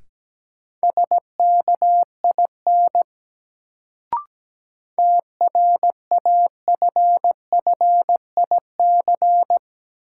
Morse code, heard as a keyed tone, about 17 wpm, 700 Hz. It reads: SKIN TRAFFIC
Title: Sentences from Top 1300 Words - New Word Drill - 17wpm